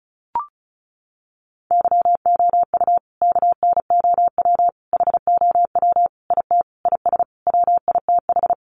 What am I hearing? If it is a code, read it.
Morse code, heard as a keyed tone, about 34 wpm, 700 Hz. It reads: YOU KNOW HOW IT IS WITH
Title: Sentences from Top 100 Words 35wpm